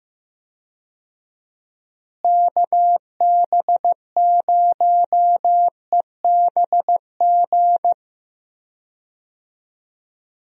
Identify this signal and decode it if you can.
Morse code, heard as a keyed tone, about 15 wpm, 700 Hz. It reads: KB0EBG
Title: USA Call Signs - 6 Characters - Repeated 2x - 15wpm